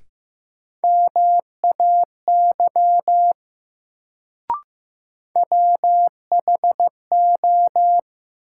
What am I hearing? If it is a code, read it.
Morse code, heard as a keyed tone, about 15 wpm, 700 Hz. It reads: MAY WHO
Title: Sentences from Top 100 Words - New Word Drill - 15wpm